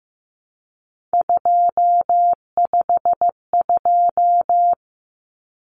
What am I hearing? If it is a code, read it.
Morse code, heard as a keyed tone, about 15 wpm, 700 Hz. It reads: 252
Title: Common Cut Numbers - 3-Digit 15wpm